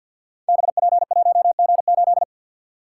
Morse code, heard as a keyed tone, about 49 wpm, 700 Hz. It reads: BP1G8
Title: Code Groups - 5 Letter:Numbers - 50wpm